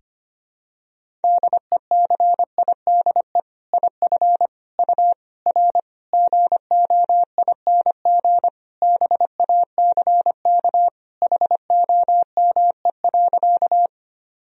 Morse code, heard as a keyed tone, about 25 wpm, 700 Hz. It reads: DECIDE IF U R GOING BACK HOME.
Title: Sentences from Top 500 Words and Abbreviations 25wpm